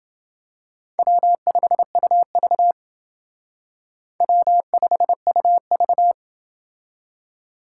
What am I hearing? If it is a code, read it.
Morse code, heard as a keyed tone, about 28 wpm, 700 Hz. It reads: W5UV W5UV